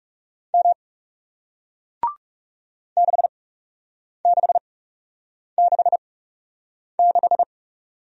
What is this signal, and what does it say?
Morse code, heard as a keyed tone, about 38 wpm, spaced as for 8 wpm, 700 Hz. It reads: M 6666